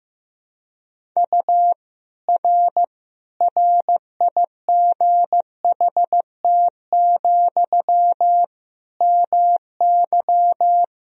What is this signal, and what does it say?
Morse code, heard as a keyed tone, about 15 wpm, 700 Hz. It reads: U R RIGHT, MY
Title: Sentences from Top 400 Words and Abbreviations 15wpm